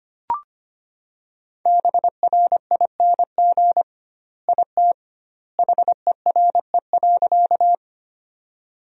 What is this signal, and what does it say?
Morse code, heard as a keyed tone, about 25 wpm, 700 Hz. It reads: BRING IT HERE.
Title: 2x Word-Spacing - Sentences from Top 500 Words - 25wpm